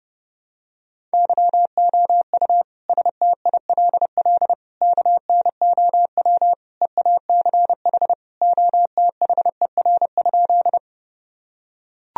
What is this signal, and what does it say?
Morse code, heard as a keyed tone, about 32 wpm, 700 Hz. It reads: YOU STILL KNOW EACH OTHER?
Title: Sentences from Top 1000 Words - Review Part 1 - 30wpm